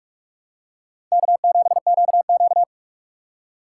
Morse code, heard as a keyed tone, about 46 wpm, 700 Hz. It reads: K7QQ
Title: USA Call Signs - 3x Character Spacing - 45wpm